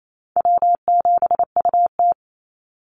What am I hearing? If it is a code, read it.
Morse code, heard as a keyed tone, about 28 wpm, 700 Hz. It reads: W7UT
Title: Easy Call Signs - 28wpm